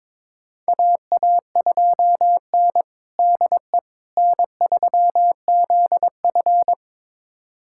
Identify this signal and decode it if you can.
Morse code, heard as a keyed tone, about 22 wpm, 700 Hz. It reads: AA2N DE N3ZF
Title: USA Callsign DE Callsign - 4 Characters - 22wpm